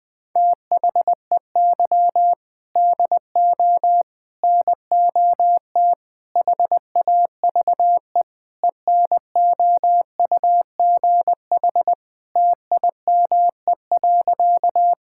Morse code, heard as a keyed tone, about 20 wpm, 700 Hz. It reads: THEY DO NOT HAVE ENOUGH TIME.
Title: Sentences with 4-Word Phrases - 20wpm